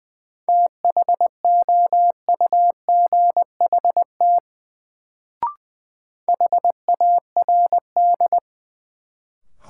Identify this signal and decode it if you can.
Morse code, heard as a keyed tone, about 20 wpm, 700 Hz. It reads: THOUGHT HARD
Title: Sentences from Top 300 Words - New Word Drill - 20wpm